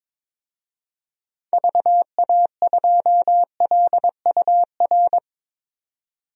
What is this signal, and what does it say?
Morse code, heard as a keyed tone, about 22 wpm, 700 Hz. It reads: VA2LUR
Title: DXCC-100 Call Signs - 6 Characters - 22wpm